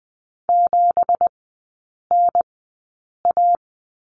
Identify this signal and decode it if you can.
Morse code, heard as a keyed tone, about 20 wpm, spaced as for 8 wpm, 690 Hz. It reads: 7NA
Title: 205 Lesson 33e - Letter j - Three Characters 20wpm